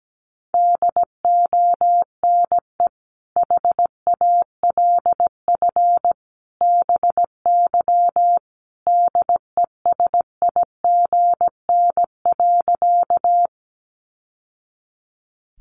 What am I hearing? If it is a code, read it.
Morse code, heard as a keyed tone, about 17 wpm, 700 Hz. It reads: DONE HALF BY DESIGN.
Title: Sentences from Top 800 Words 17wpm